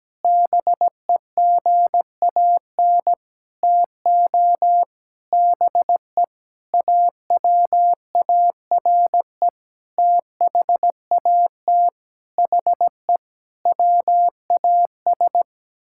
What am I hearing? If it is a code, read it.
Morse code, heard as a keyed tone, about 17 wpm, 700 Hz. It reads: BEGAN TO BE AWARE THAT HE WAS
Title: Sentences from Top 1600 Words 17wpm